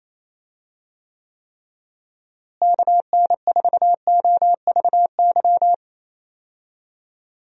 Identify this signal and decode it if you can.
Morse code, heard as a keyed tone, about 28 wpm, 700 Hz. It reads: KN4OVY
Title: USA Call Signs - 6 Characters - Repeated 3x - 28wpm